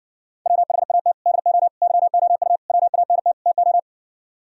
Morse code, heard as a keyed tone, about 49 wpm, 700 Hz. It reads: WHAT DO YOU WANT TO
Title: Speed-Racing - Sets of 5 Words 40wpm